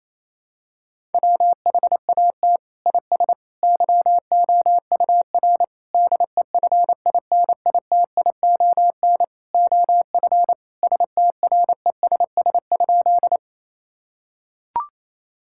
Morse code, heard as a keyed tone, about 27 wpm, 700 Hz. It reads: WHAT IS YOUR DEFINITION OF STRESS?
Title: Sentences from Top 1000 Words - Review Part 6 - 28wpm